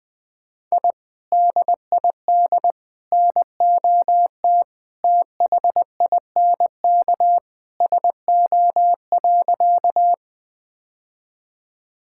Morse code, heard as a keyed tone, about 21 wpm, 700 Hz. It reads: I DID NOT THINK SO.
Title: Sentences with 2-Word Phrases - 20wpm